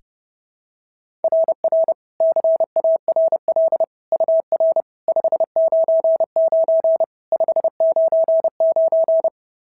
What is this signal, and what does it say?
Morse code, heard as a keyed tone, about 26 wpm, 650 Hz. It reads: RR CARL UR 599 599
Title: SOTA QSO Practice 30wpm